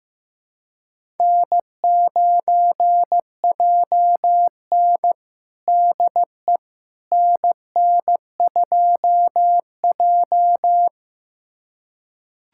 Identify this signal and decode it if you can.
Morse code, heard as a keyed tone, about 15 wpm, 700 Hz. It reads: N9JN DE NN2J
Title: USA Callsign DE Callsign - 4 Characters - 15wpm